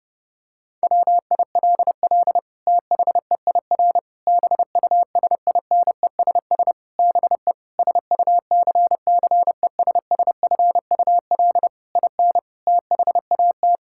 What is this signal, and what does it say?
Morse code, heard as a keyed tone, about 30 wpm, 700 Hz. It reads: WILL THEIR BUSINESS BE SUCCESSFUL IN THAT
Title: Sentences from Top 1000 Words - Review Part 5 - 30wpm